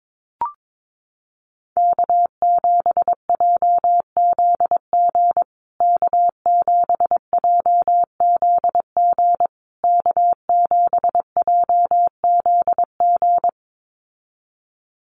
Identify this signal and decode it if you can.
Morse code, heard as a keyed tone, about 22 wpm, 700 Hz. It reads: K7JZG K7JZG K7JZG
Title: USA Call Signs Repeated 3x 22wpm